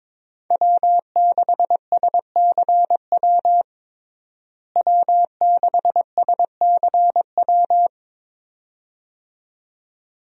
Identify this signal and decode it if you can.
Morse code, heard as a keyed tone, about 22 wpm, 700 Hz. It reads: W6SCW W6SCW
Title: USA Call Signs - 5 Characters - Repeated 2x - 22wpm